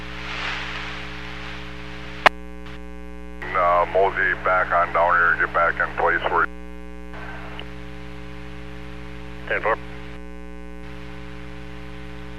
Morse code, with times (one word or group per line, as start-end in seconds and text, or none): none